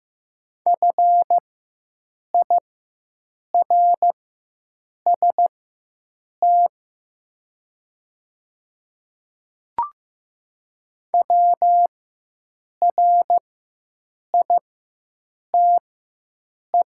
Morse code, heard as a keyed tone, about 15 wpm, spaced as for 7 wpm, 700 Hz. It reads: FIRST WRITE
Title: 4x Character Spacing - Top 100 Words - 15wpm